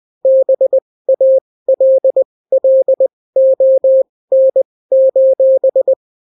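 Morse code, heard as a keyed tone, about 20 wpm, spaced as for 16 wpm, 530 Hz. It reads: BALLON:
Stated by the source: Daily Morse weather and top stories at 16 wpm, clean without QRN.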